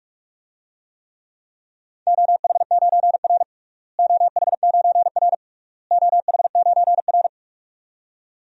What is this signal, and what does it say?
Morse code, heard as a keyed tone, about 46 wpm, 700 Hz. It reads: OH9R OH9R OH9R
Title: DXCC-100 Call Signs - 4 Characters - Repeated 3x - 45wpm